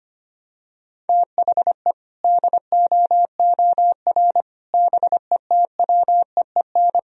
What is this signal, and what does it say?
Morse code, heard as a keyed tone, about 25 wpm, 700 Hz. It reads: THE DOOR BETWEEN
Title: Sentences from Top 1000 Words - Review Part 1 - 25wpm